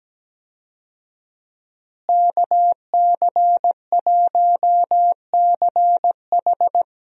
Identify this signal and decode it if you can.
Morse code, heard as a keyed tone, about 17 wpm, 700 Hz. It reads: KC1CH
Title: USA Call Signs - 5 Characters - Repeated 3x - 17wpm